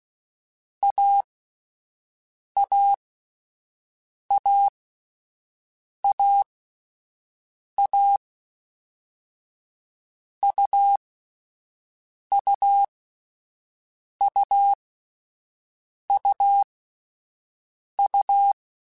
Morse code, heard as a keyed tone, about 16 wpm, spaced as for 5 wpm, 800 Hz. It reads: AAAAA UUUUU
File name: Difficult Combos / Difficult Combos-mp3s / AUV AUV4 @5